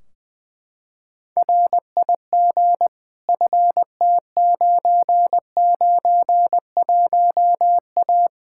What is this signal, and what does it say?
Morse code, heard as a keyed tone, about 20 wpm, 700 Hz. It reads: RIG FT991A
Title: Practice QSO - Exchange 2 - Info - 20wpm